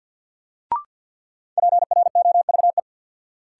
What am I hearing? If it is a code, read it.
Morse code, heard as a keyed tone, about 50 wpm, 700 Hz. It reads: PROVE
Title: Sentences from Top 1600 Words - New Word Drill Alt - 50wpm